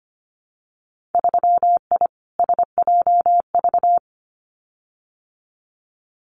Morse code, heard as a keyed tone, about 25 wpm, 700 Hz. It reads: 3I SJV
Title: Winter Field Day Exchanges Encore 25wpm